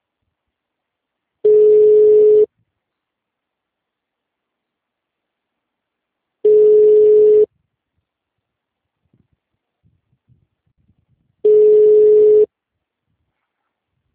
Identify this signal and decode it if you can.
Ringback tone (Europe, 425 Hz).
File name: out- -2016- - - .5204